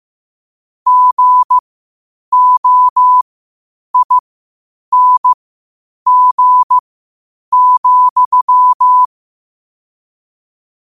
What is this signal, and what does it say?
Morse code, heard as a keyed tone, about 15 wpm, spaced as for 8 wpm, 1000 Hz. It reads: GOING,
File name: qotdfiles / 8wpm